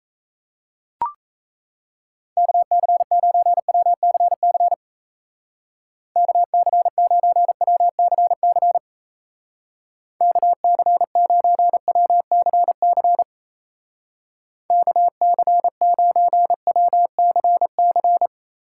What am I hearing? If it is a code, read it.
Morse code, heard as a keyed tone, about 34 wpm, 700 Hz. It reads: KC9WCC KC9WCC KC9WCC KC9WCC
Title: Speed Racing - USA 2x3 Call Signs 28wpm